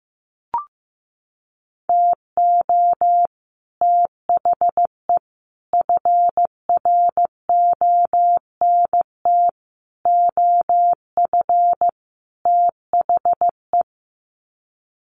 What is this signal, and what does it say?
Morse code, heard as a keyed tone, about 15 wpm, 700 Hz. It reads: TO THE FRONT OF THE
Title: Sets of 5 Words 15wpm